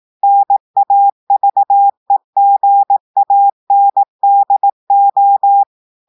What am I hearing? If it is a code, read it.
Morse code, heard as a keyed tone, about 18 wpm, 800 Hz. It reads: NAVEGANDO
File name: audios / portugais / portugais texte en clair 1 vitesse 18 mots minutes